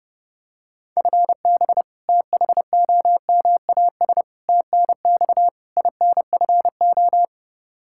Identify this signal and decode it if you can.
Morse code, heard as a keyed tone, about 30 wpm, 700 Hz. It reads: FB THOMAS TNX INFO
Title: Practice QSO - Exchange 3 - More Info - 30wpm